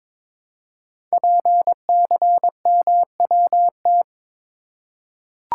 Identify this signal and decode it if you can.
Morse code, heard as a keyed tone, about 22 wpm, 700 Hz. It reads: PCMWT
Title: Code Groups - 5 Letters - 22wpm